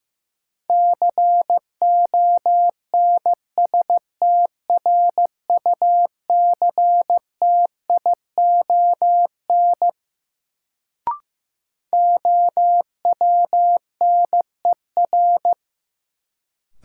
Morse code, heard as a keyed tone, about 15 wpm, 700 Hz. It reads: CONSTRUCTION OWNER
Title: Sentences from Top 1600 Words - New Word Drill Alt - 15wpm